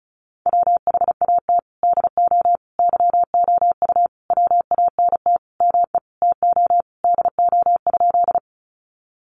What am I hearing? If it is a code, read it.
Morse code, heard as a keyed tone, about 34 wpm, 700 Hz. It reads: WHAT DO YOU WANT ME TO DO?
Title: Sentences from Top 200 Words 35wpm